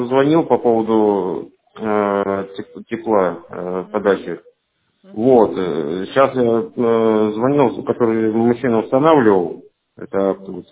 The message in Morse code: none

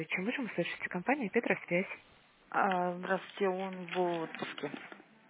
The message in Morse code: none

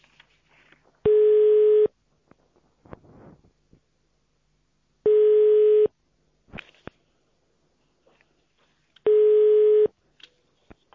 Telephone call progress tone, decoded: ringback tone

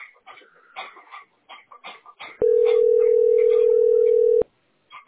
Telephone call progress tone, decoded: ringback tone

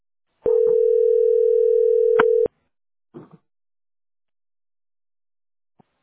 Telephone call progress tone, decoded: ringback tone

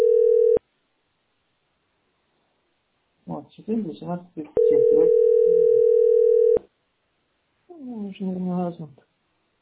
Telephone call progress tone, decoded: ringback tone

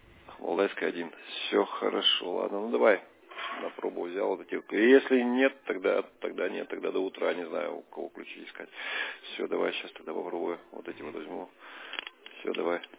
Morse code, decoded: none

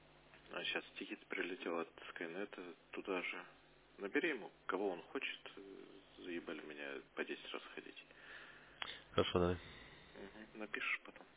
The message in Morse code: none

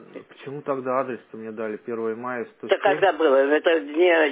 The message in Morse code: none

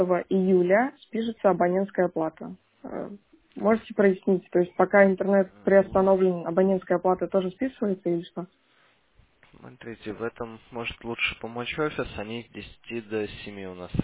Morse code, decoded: none

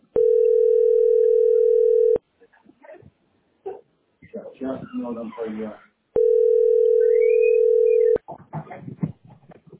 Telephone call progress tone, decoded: ringback tone

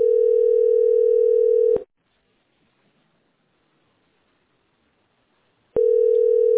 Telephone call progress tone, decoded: ringback tone

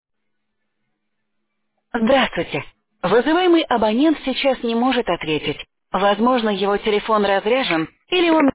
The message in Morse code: none

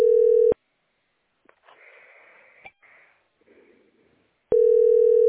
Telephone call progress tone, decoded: ringback tone

